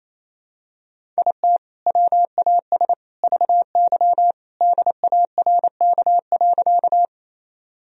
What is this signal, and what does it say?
Morse code, heard as a keyed tone, about 28 wpm, 700 Hz. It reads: IT WAS VY DARK.
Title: Sentences from Top 500 Words and Abbreviations 28wpm